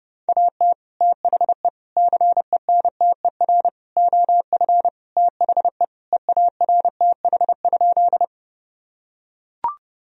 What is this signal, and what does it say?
Morse code, heard as a keyed tone, about 30 wpm, 700 Hz. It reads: AT THE CENTER OF THE EARTH?
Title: Sentences from Top 1000 Words - Review Part 2 - 30wpm